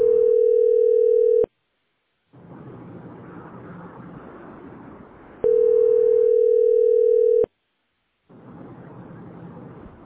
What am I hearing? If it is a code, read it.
Ringback tone (North America), 440 + 480 Hz.